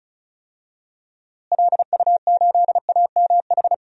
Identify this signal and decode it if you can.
Morse code, heard as a keyed tone, about 34 wpm, 700 Hz. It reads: LU8AMH
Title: DXCC-100 Call Signs - 6 Characters - Repeated 2x - 35wpm